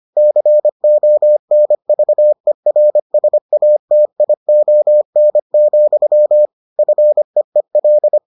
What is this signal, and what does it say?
Morse code, heard as a keyed tone, about 25 wpm, 600 Hz. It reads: CONVERSATION, FEEL